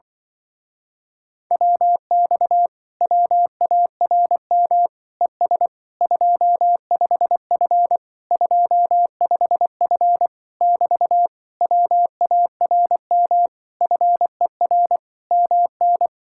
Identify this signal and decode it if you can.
Morse code, heard as a keyed tone, about 24 wpm, 700 Hz. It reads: WX WARM ES 25F 25F <BT> WARM FER MN
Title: Speed-Racing Common QSO Phrases Encore - 20wpm